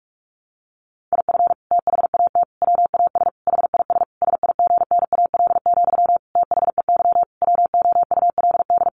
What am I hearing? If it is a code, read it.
Morse code, heard as a keyed tone, about 44 wpm, 700 Hz. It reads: IF THAT WAS HIS SIGNAL, THEY WOULD
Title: Sentences from Top 1400 Words 45wpm